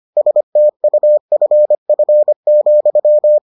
Morse code, heard as a keyed tone, about 25 wpm, 600 Hz. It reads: STUFF,